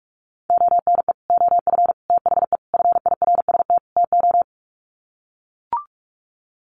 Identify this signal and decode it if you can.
Morse code, heard as a keyed tone, about 44 wpm, 700 Hz. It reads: ONE OF THE FIRST TO
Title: Sets of 5 Words Encore 45wpm